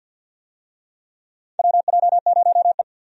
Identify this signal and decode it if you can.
Morse code, heard as a keyed tone, about 53 wpm, 700 Hz. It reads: WJ0E
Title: USA Call Signs - 4 Characters - 50wpm